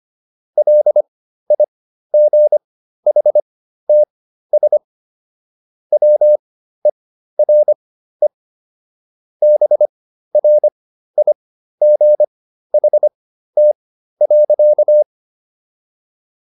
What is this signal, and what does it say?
Morse code, heard as a keyed tone, about 25 wpm, spaced as for 13 wpm, 600 Hz. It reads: LIGHTS WERE BRIGHT.